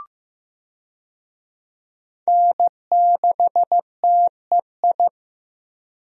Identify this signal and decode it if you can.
Morse code, heard as a keyed tone, about 15 wpm, 700 Hz. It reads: N6TEI
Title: USA Call Signs - 5 Characters - Repeated 2x - 15wpm